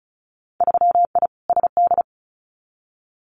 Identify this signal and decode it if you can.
Morse code, heard as a keyed tone, about 36 wpm, 700 Hz. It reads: 3I SD
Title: Winter Field Day Exchanges Encore 35wpm